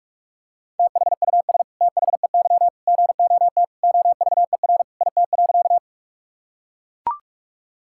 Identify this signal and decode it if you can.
Morse code, heard as a keyed tone, about 43 wpm, 700 Hz. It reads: THUS THEY GOT OVER IT.
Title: Sentences from Top 1000 Words - Review Part 5 - 45wpm